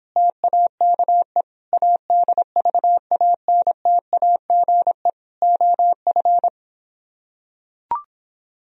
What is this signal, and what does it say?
Morse code, heard as a keyed tone, about 26 wpm, 700 Hz. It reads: TAKE ADVANTAGE OF